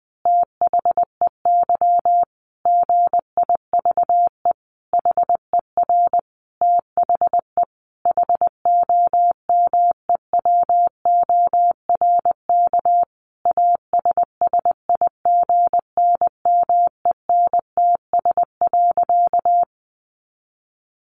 Morse code, heard as a keyed tone, about 20 wpm, 700 Hz. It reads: THEY GIVE HER THE HOMEWORK ASSIGNMENTS.
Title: Easy Phrases 20wpm